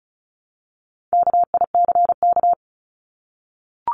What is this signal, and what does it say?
Morse code, heard as a keyed tone, about 34 wpm, 700 Hz. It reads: KICK
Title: Top 1251-1500 Words 35wpm